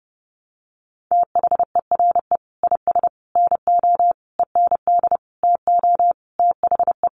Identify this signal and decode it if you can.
Morse code, heard as a keyed tone, about 28 wpm, 700 Hz. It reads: THERE IS NO END TO THE
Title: Sentences from Top 1200 Words 30wpm